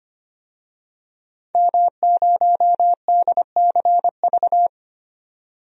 Morse code, heard as a keyed tone, about 25 wpm, 700 Hz. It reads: M0DCV
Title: DXCC-100 Call Signs - 5 Characters - 25wpm